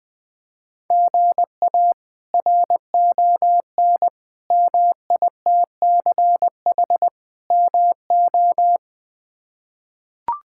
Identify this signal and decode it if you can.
Morse code, heard as a keyed tone, about 20 wpm, 700 Hz. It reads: GA RON MITCH MO
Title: K1USN SST QSO Practice - 20wpm